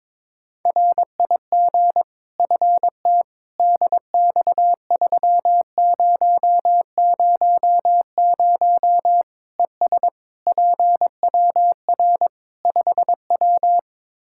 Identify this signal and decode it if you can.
Morse code, heard as a keyed tone, about 22 wpm, 700 Hz. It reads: RIG FT DX3000 ES PWR 5W
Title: Practice QSO - Exchange 2 - Info - 22wpm